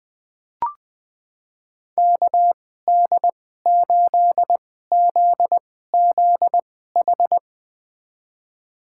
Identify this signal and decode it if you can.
Morse code, heard as a keyed tone, about 20 wpm, spaced as for 14 wpm, 700 Hz. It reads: KD8ZZH
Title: USA Call Signs - 6 Characters - 2x Character Spacing - 20wpm